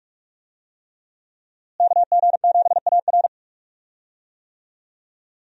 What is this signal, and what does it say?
Morse code, heard as a keyed tone, about 46 wpm, 700 Hz. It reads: KG7AR
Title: USA Call Signs - 5 Characters - Repeated 3x - 45wpm